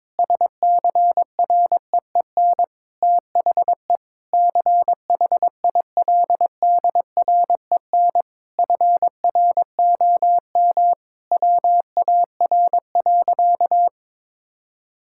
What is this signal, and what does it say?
Morse code, heard as a keyed tone, about 22 wpm, 700 Hz. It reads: SCREEN THE CHILDREN FROM WAR.
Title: Sentences from Top 1000 Words - Review Part 6 - 22wpm